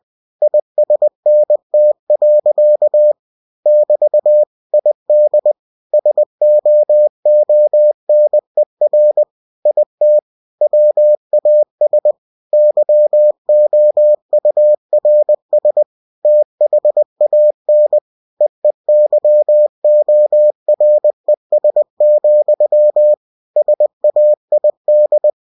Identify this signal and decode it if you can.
Morse code, heard as a keyed tone, about 20 wpm, 600 Hz. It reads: ISNT. <BT> ID SOONER IT WAS YOURS THAN EEYORES, SAID
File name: cw-books / winnie-the-pooh / winnie-20wpm-page-0108